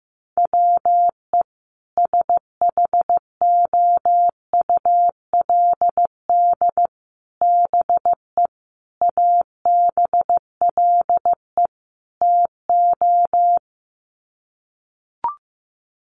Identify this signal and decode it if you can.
Morse code, heard as a keyed tone, about 15 wpm, 700 Hz. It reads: WE SHOULD BE ABLE TO
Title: Sets of 5 Words Encore2 15wpm